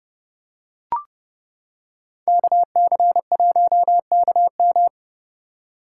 Morse code, heard as a keyed tone, about 30 wpm, 700 Hz. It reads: KC1KM